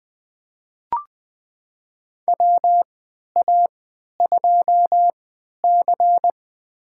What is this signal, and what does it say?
Morse code, heard as a keyed tone, about 20 wpm, spaced as for 11 wpm, 700 Hz. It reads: WA2C
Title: USA Call Signs - 4 Characters - 3x Character Spacing - 20wpm